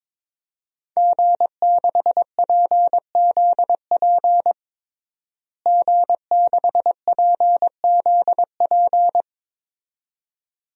Morse code, heard as a keyed tone, about 22 wpm, 700 Hz. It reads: G6PZP G6PZP
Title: DXCC-100 Call Signs - 5 Characters - Repeated 2x - 22wpm